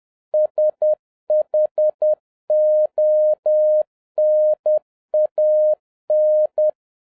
Morse code, heard as a keyed tone, about 10 wpm, 610 Hz. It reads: SHONAN
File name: 2011-07-mp3 / n31 THE 3.11 2 10wpm